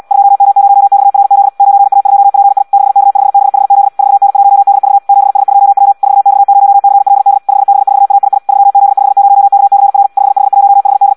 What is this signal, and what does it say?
Morse code, heard as a keyed tone, about 74 wpm, 800 Hz. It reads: 2M0XRZ 9A2PI 6K5BHZ 4N1DV 7S2W 4X0CLD 5B4AII 3Z50KCR 5H1HD
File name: Audio practica CW